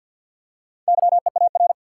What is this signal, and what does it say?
Morse code, heard as a keyed tone, about 52 wpm, 700 Hz. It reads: YEAR
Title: Four Letter Words - 50wpm